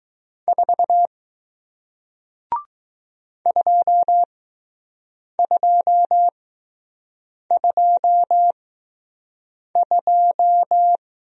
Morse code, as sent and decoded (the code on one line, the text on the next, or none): ....- / ..--- ..--- ..--- ..---
4 2222